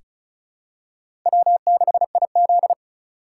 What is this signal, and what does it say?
Morse code, heard as a keyed tone, about 34 wpm, 700 Hz. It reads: W6IZ
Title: USA Call Signs - 4 Characters - 4x Character Spacing - 35wpm